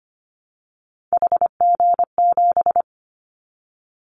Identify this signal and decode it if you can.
Morse code, heard as a keyed tone, about 25 wpm, 700 Hz. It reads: HG7
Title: Callsign Prefixes for Top 50 DXCC - 25wpm